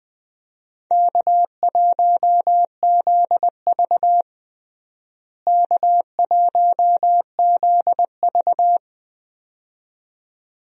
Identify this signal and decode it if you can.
Morse code, heard as a keyed tone, about 20 wpm, 700 Hz. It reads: K1ZV K1ZV
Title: USA Call Signs - 4 Characters - Repeated 2x - 20wpm